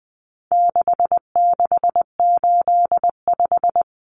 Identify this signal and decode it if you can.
Morse code, heard as a keyed tone, about 20 wpm, 700 Hz. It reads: 6685
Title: Common Cut Numbers - 4-Digit 20wpm